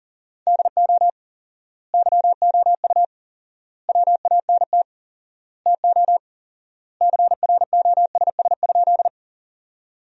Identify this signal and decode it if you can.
Morse code, heard as a keyed tone, about 40 wpm, 700 Hz. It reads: DO YOU WANT TO CROSS?
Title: 4x Word-Spacing - Sentences from Top 300 Words - 40wpm